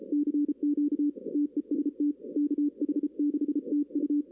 Morse code, heard as a keyed tone, about 33 wpm, 290 Hz. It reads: CQ TEST KH6TU
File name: 2016 arrl dx cw / 2016-ARRL-DX-CW-K5TR- - 0-9